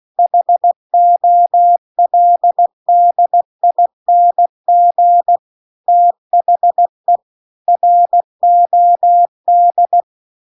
Morse code, heard as a keyed tone, about 16 wpm, 700 Hz. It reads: HOLDING THE ROD
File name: CW-The-Big-Twoheaded-River-2-16WPM